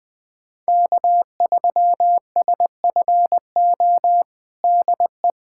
Morse code, heard as a keyed tone, about 20 wpm, 700 Hz. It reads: K3SFO DE
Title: USA Callsign DE Callsign - 5 Characters - 20wpm